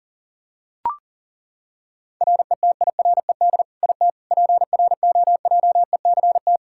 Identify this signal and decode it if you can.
Morse code, heard as a keyed tone, about 40 wpm, 700 Hz. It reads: RETIRED IT PROJECT
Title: Practice QSO - Exchange 3 - More Info - 40wpm